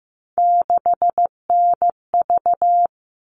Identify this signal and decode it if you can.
Morse code, heard as a keyed tone, about 15 wpm, 700 Hz. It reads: 6NV
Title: All Cut Numbers - 3-Digit 15wpm